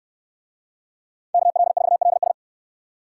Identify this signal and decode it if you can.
Morse code, heard as a keyed tone, about 61 wpm, 700 Hz. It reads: KL4LS